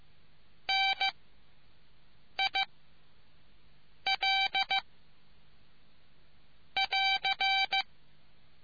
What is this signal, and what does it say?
Morse code, heard as a keyed tone, about 15 wpm, 780 Hz. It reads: N I L <AR>